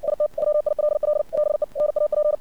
Morse code, heard as a keyed tone, about 57 wpm, 600 Hz. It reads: IT WILL BE NAP
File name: archived / CRSnet